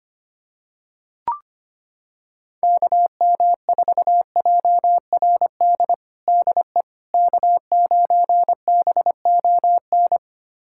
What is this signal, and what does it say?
Morse code, heard as a keyed tone, about 25 wpm, 700 Hz. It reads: KM4JRD DE K9BON
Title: USA Callsign DE Callsign 25wpm